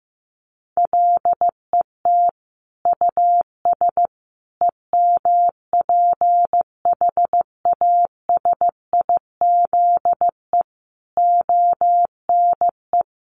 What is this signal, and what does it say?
Morse code, heard as a keyed tone, about 15 wpm, 700 Hz. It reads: LET US EMPHASIZE ONE
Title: Sentences from Top 1500 Words 15wpm